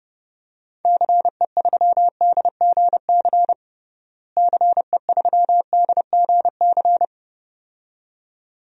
Morse code, heard as a keyed tone, about 31 wpm, 700 Hz. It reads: CE3DGC CE3DGC
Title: DXCC-100 Call Signs - 6 Characters - Repeated 2x - 30wpm